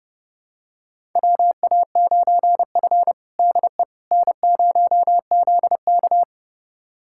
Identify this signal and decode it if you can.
Morse code, heard as a keyed tone, about 30 wpm, 700 Hz. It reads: WA9F DE N0ZK
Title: USA Callsign DE Callsign - 4 Characters - 30wpm